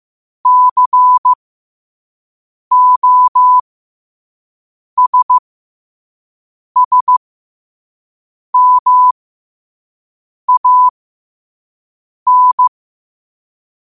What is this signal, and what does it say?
Morse code, heard as a keyed tone, about 15 wpm, spaced as for 5 wpm, 1000 Hz. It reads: COSSMAN
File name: qotdfiles / 5wpm